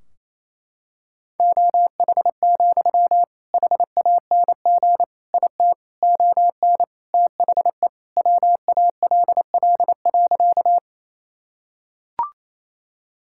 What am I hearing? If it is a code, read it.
Morse code, heard as a keyed tone, about 28 wpm, 700 Hz. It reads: OH, HANG IT ON THE WALL.
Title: Sentences from Top 1600 Words 28wpm